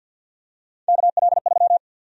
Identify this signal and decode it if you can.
Morse code, heard as a keyed tone, about 52 wpm, 700 Hz. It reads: KL3
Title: USA Call Sign Prefixes - 50wpm